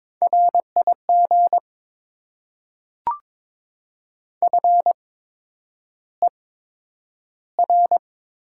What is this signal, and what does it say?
Morse code, heard as a keyed tone, about 22 wpm, spaced as for 9 wpm, 700 Hz. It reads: RIG FER
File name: Top 20 QSO Elements - 8x Character Spacing - 22wpm